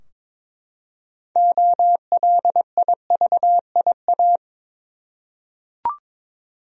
Morse code, heard as a keyed tone, about 22 wpm, 700 Hz. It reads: OLIVIA
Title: USA Common Names - 22wpm